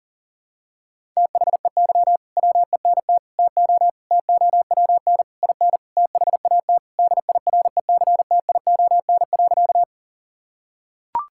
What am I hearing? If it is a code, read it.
Morse code, heard as a keyed tone, about 40 wpm, 700 Hz. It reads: THEY WENT TO TOWN IN THAT DIRECTION.
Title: Sentences from Top 1000 Words - Review Part 5 - 40wpm